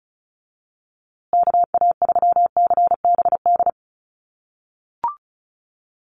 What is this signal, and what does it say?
Morse code, heard as a keyed tone, about 34 wpm, 700 Hz. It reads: KA3CBD
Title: USA Call Signs Repeated 3x 35wpm